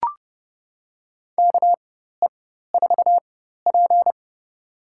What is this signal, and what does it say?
Morse code, heard as a keyed tone, about 30 wpm, spaced as for 14 wpm, 700 Hz. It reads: KE4P
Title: USA Call Signs - 4x Character Spacing - 30wpm